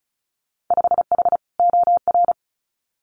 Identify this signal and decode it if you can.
Morse code, heard as a keyed tone, about 34 wpm, 700 Hz. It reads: 5H OR
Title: Winter Field Day Exchanges Encore 35wpm